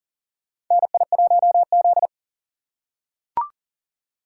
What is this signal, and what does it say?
Morse code, heard as a keyed tone, about 37 wpm, 700 Hz. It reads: NI1Z
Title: USA Call Signs - 4 Characters - 3x Character Spacing - 40wpm